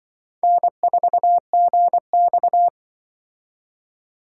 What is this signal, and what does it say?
Morse code, heard as a keyed tone, about 24 wpm, 700 Hz. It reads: N4GX
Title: Speed Racing - USA 1x2 Call Signs 20wpm